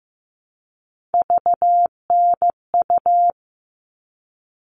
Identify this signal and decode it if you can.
Morse code, heard as a keyed tone, about 15 wpm, 700 Hz. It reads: VNU
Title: All Cut Numbers - 3-Digit 15wpm